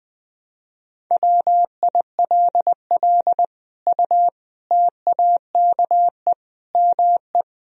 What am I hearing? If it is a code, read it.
Morse code, heard as a keyed tone, about 20 wpm, 700 Hz. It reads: WILL U TAKE ME
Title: Sentences from Top 200 Words and Abbreviations 20wpm